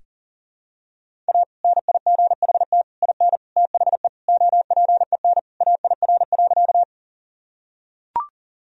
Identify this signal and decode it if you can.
Morse code, heard as a keyed tone, about 40 wpm, 700 Hz. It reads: A NIGHT IN THE OPEN AIR.
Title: Sentences from Top 1000 Words - Review Part 1 - 40wpm